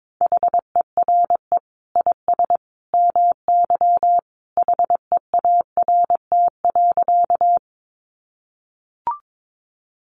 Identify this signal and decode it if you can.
Morse code, heard as a keyed tone, about 22 wpm, 700 Hz. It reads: HERE IS MY HEART.
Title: Sentences from Top 1200 Words 22wpm